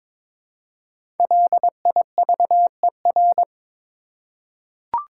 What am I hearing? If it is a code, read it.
Morse code, heard as a keyed tone, about 22 wpm, 700 Hz. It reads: LIVER
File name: 5 Letter Words - First 3 Letters Given - 22wpm